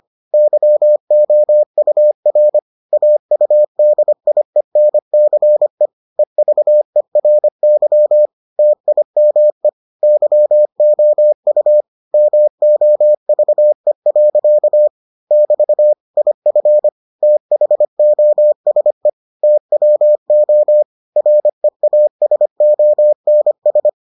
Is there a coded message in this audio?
Yes: Morse code from 0.3 to 23.9 s, heard as a keyed tone, about 25 wpm, 600 Hz. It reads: YOUR AUDIENCE EVERY TIME YOU MOVE. <BT> IF THOSE TWO REASONS